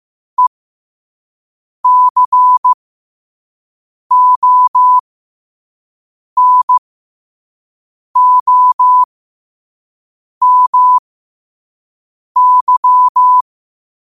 Morse code, heard as a keyed tone, about 15 wpm, spaced as for 5 wpm, 1000 Hz. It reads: ECONOMY
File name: qotdfiles / 5wpm